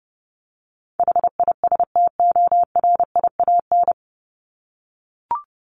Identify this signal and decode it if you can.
Morse code, heard as a keyed tone, about 30 wpm, 700 Hz. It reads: HISTORIAN
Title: Top 1751-2000 Words 30wpm